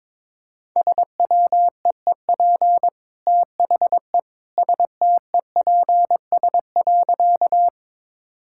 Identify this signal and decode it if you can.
Morse code, heard as a keyed tone, about 22 wpm, 700 Hz. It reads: SWEEP THE STEPS.
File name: practice / 22wpm / Easy Sentences with Hints - 22wpm